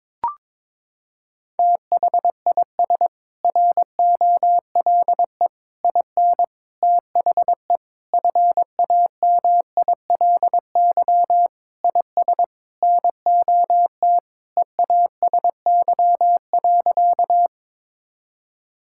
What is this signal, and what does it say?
Morse code, heard as a keyed tone, about 22 wpm, 700 Hz. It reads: THIS ROLE IN THE FAMILY IS NOT EASY.
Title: Sentences from Top 1000 Words - Review Part 6 - 22wpm